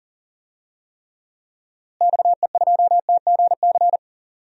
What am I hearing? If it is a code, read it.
Morse code, heard as a keyed tone, about 40 wpm, 700 Hz. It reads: XE2TGC
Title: DXCC-100 Call Signs - 6 Characters - Repeated 3x - 40wpm